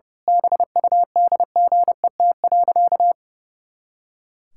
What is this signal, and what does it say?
Morse code, heard as a keyed tone, about 30 wpm, 700 Hz. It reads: BUDGET.
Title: Pangram Challenge v2 - 30wpm